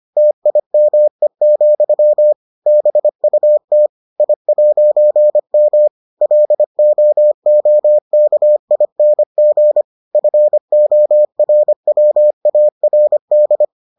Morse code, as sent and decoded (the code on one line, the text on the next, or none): - .. -- . --..-- / -... ..- - / .. .----. -- / .-.. --- --- -.- .. -. --. / ..-. --- .-. .-- .- .-. -..
TIME, BUT I'M LOOKING FORWARD